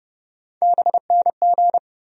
Morse code, heard as a keyed tone, about 30 wpm, 700 Hz. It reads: BNG
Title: USA Call Sign Postfixes - 3 Letters 30wpm